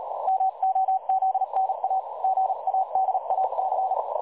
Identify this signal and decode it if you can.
Morse code, heard as a keyed tone, about 38 wpm, 760 Hz. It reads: MOZNA GADAC
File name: dlazaawansowanych / nagraniazpasma / websdr recording 3565.0kHz